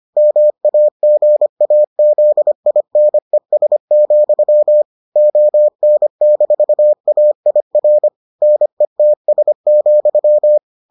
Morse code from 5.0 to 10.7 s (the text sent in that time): ON-AIR NETS,